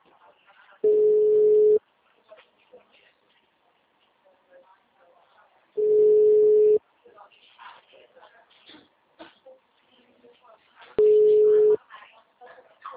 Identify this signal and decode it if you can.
Ringback tone (Europe, 425 Hz), with speech.
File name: out- -2004- - - .4082